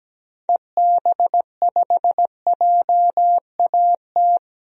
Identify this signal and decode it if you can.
Morse code, heard as a keyed tone, about 17 wpm, 700 Hz. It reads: EB5JAT
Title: DXCC-100 Call Signs - 6 Characters - 17wpm